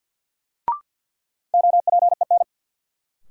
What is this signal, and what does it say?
Morse code, heard as a keyed tone, about 53 wpm, 700 Hz. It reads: OPEN
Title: Sentences from Top 300 Words - New Word Drill - 50wpm